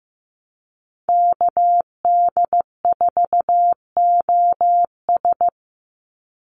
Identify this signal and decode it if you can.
Morse code, heard as a keyed tone, about 15 wpm, 700 Hz. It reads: KD4OS
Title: Call Signs Repeated 3x - 15wpm